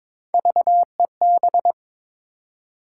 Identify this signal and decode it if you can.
Morse code, heard as a keyed tone, about 22 wpm, 700 Hz. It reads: VEB
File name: practice / 22wpm / Three Letter-Number Combinations - 22wpm